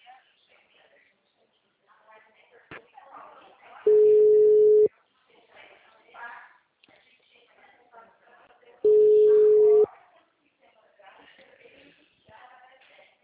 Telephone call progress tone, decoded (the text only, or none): ringback tone